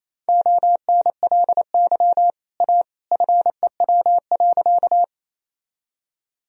Morse code, heard as a keyed tone, about 28 wpm, 700 Hz. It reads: ONLY A FEW.
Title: Sentences from Top 1000 Words - Review Part 1 - 28wpm